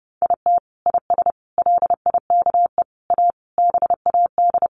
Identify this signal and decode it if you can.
Morse code, heard as a keyed tone, about 30 wpm, 700 Hz. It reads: IT IS LIKE A BAD
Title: Sentences from Top 1500 Words 30wpm